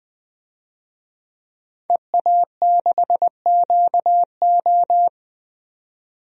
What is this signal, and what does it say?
Morse code, heard as a keyed tone, about 21 wpm, 700 Hz. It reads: EA6QO
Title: DXCC-100 Call Signs - 5 Characters - 20wpm